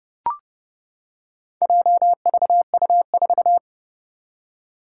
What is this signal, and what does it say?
Morse code, heard as a keyed tone, about 30 wpm, 700 Hz. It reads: JVU4
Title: Four Challenging Letter-Number Combinations 30wpm